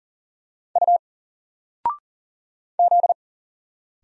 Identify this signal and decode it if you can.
Morse code, heard as a keyed tone, about 40 wpm, 700 Hz. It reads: U Z
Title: Single Letter-Number 40wpm